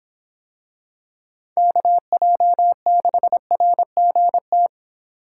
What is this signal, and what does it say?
Morse code, heard as a keyed tone, about 25 wpm, 700 Hz. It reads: KJ6RGT